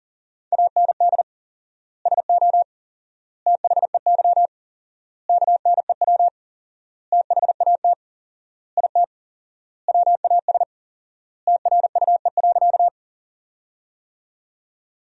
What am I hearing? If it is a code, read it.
Morse code, heard as a keyed tone, about 37 wpm, 700 Hz. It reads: AND SO THEY KNEW THAT IT WAS TRUE.
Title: 4x Word-Spacing - Sentences from Top 400 Words - 40wpm